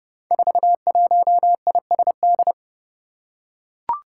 Morse code, heard as a keyed tone, about 30 wpm, 700 Hz. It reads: 41ISD
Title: Five Challenging Letter-Number Combinations 30wpm